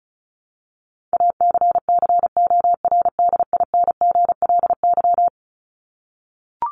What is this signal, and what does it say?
Morse code, heard as a keyed tone, about 34 wpm, 700 Hz. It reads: ACCORDINGLY
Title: LY Suffix - Top 100 Words with Base Words 35wpm